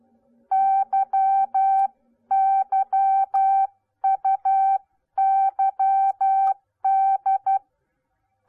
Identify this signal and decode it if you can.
Morse code, heard as a keyed tone, about 12 wpm, 780 Hz. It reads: YYUYD